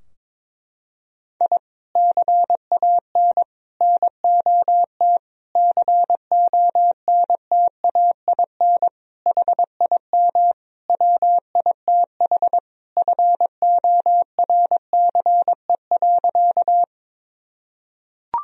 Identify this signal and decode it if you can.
Morse code, heard as a keyed tone, about 22 wpm, 700 Hz. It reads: I CAN NOT CONTAIN HIM WITH FORCE.
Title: Sentences from Top 1000 Words - Review Part 3 - 22wpm